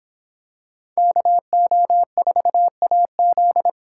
Morse code, heard as a keyed tone, about 24 wpm, 700 Hz. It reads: KO4AZ